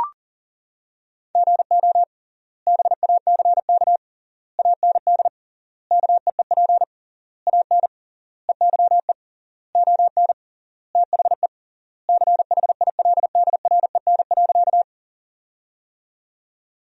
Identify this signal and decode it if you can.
Morse code, heard as a keyed tone, about 37 wpm, 700 Hz. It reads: GO BACK AND KEEP AN EYE ON THE CHILDREN.
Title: 3x Word-Spacing - Sentences from Top 300 Words - 40wpm